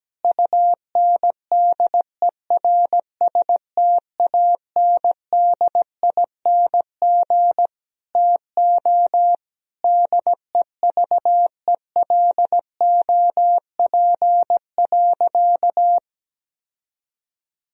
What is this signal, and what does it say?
Morse code, heard as a keyed tone, about 17 wpm, 700 Hz. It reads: UNDERSTANDING TO DEVELOP.
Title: Sentences from Top 1000 Words - Review Part 3 - 17wpm